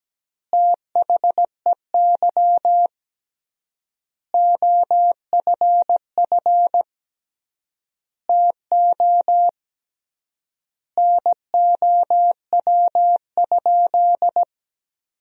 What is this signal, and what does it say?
Morse code, heard as a keyed tone, about 17 wpm, 700 Hz. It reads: THEY OFF TO NOW?
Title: 3x Word-Spacing - Sentences from Top 200 Words - 17wpm